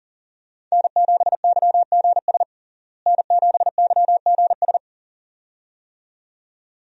Morse code, heard as a keyed tone, about 40 wpm, 700 Hz. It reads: N7YGS N7YGS
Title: USA Call Signs - 5 Characters - Repeated 2x - 40wpm